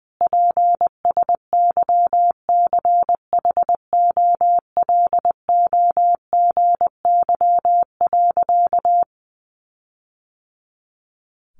Morse code, heard as a keyed tone, about 20 wpm, 700 Hz. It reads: PSYCHOLOGY.
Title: Sentences from Top 1400 Words 20wpm